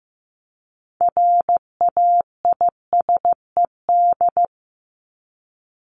Morse code, heard as a keyed tone, about 15 wpm, 700 Hz. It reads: RAISED
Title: ED Suffix - Top 100 Words with Base Words 15wpm